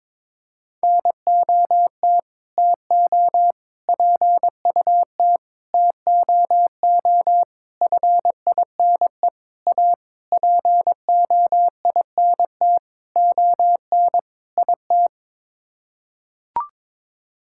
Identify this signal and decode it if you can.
Morse code, heard as a keyed tone, about 22 wpm, 700 Hz. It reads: NOT TO PUT TOO FINE A POINT ON IT
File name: Sets of 9 Words - 22wpm